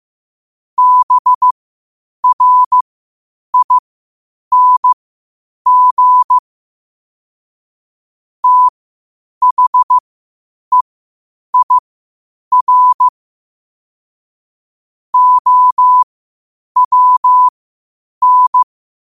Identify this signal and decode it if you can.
Morse code, heard as a keyed tone, about 15 wpm, spaced as for 8 wpm, 1000 Hz. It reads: BRING THEIR OWN
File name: qotdfiles / 8wpm